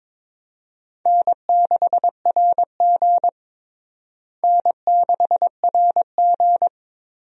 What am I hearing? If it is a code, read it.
Morse code, heard as a keyed tone, about 22 wpm, 700 Hz. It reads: N6RG N6RG
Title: USA Call Signs - 4 Characters - Repeated 2x - 22wpm